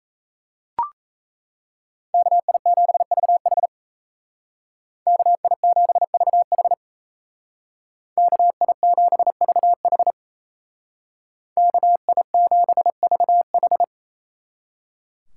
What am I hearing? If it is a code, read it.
Morse code, heard as a keyed tone, about 36 wpm, 700 Hz. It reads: KI7VH KI7VH KI7VH KI7VH